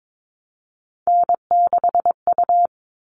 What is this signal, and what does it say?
Morse code, heard as a keyed tone, about 22 wpm, 700 Hz. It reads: N6U
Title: All Cut Numbers - 3-Digit 22wpm